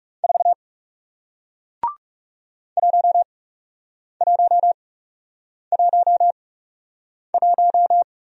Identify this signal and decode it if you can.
Morse code, heard as a keyed tone, about 40 wpm, spaced as for 8 wpm, 700 Hz. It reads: 4 1111